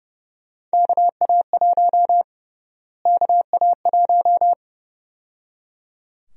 Morse code, heard as a keyed tone, about 30 wpm, 700 Hz. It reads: KA1 KA1
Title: USA Call Sign Prefixes - Repeated 2x - 30wpm